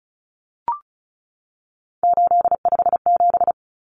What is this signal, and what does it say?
Morse code, heard as a keyed tone, about 34 wpm, 700 Hz. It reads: 857
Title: Common Cut Numbers - 3-Digit 35wpm